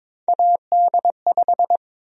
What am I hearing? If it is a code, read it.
Morse code, heard as a keyed tone, about 22 wpm, 700 Hz. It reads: AD5
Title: USA Call Sign Prefixes - 22wpm